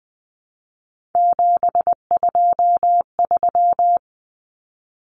Morse code, heard as a keyed tone, about 20 wpm, 700 Hz. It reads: 723
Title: Numbers - 3-Digit 20wpm